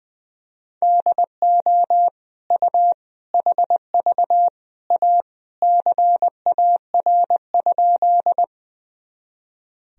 Morse code, heard as a keyed tone, about 20 wpm, 700 Hz. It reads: DO U HV A CAR?
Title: Sentences from Top 300 Words and Abbreviations 20wpm